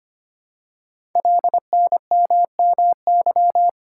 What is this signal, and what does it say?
Morse code, heard as a keyed tone, about 25 wpm, 700 Hz. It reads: LNMMY
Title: Code Groups - 5 Letters - 25wpm